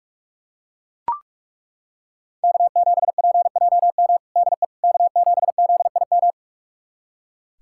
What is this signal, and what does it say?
Morse code, heard as a keyed tone, about 41 wpm, 700 Hz. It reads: K7PJM DE K7ZIM
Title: USA Callsign DE Callsign - 5 Characters - 45wpm